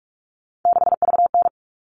Morse code, heard as a keyed tone, about 44 wpm, 700 Hz. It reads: 6VN
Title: All Cut Numbers - 3-Digit 45wpm